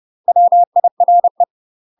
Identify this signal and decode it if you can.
Morse code, heard as a keyed tone, about 30 wpm, 700 Hz. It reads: WIRE
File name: UK MASTER QSOs TRAINING / 30WPM / UK MASTER QSOs 0030